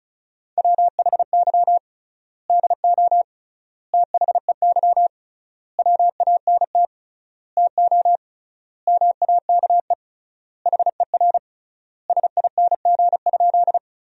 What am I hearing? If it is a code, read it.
Morse code, heard as a keyed tone, about 36 wpm, 700 Hz. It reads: WHY DO THEY WANT TO MAKE HER SING?
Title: 3x Word-Spacing - Sentences from Top 400 Words - 35wpm